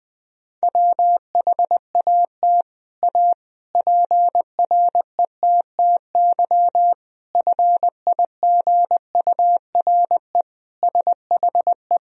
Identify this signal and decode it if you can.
Morse code, heard as a keyed tone, about 20 wpm, 700 Hz. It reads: WHAT A PRETTY FIGURE SHE
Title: Sentences from Top 1000 Words - Review Part 2 - 20wpm